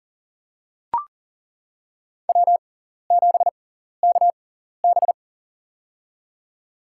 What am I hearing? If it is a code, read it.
Morse code, heard as a keyed tone, about 40 wpm, spaced as for 14 wpm, 700 Hz. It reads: W7KB
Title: USA Call Signs - 4 Characters - 6x Character Spacing - 40wpm